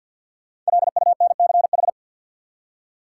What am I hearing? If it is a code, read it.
Morse code, heard as a keyed tone, about 48 wpm, 700 Hz. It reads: LUNCH
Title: Five Letter Words - 50wpm